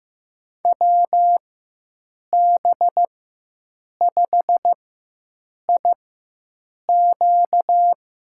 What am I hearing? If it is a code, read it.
Morse code, heard as a keyed tone, about 15 wpm, spaced as for 7 wpm, 700 Hz. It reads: WB5IQ
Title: USA Call Signs - 4x Character Spacing - 15wpm